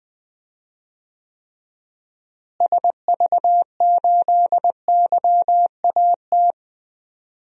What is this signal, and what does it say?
Morse code, heard as a keyed tone, about 21 wpm, 700 Hz. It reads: SV8YAT
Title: DXCC-100 Call Signs - 6 Characters - Repeated 2x - 20wpm